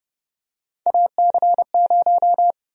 Morse code, heard as a keyed tone, about 30 wpm, 700 Hz. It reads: AC0
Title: USA Call Sign Prefixes - Repeated 2x - 30wpm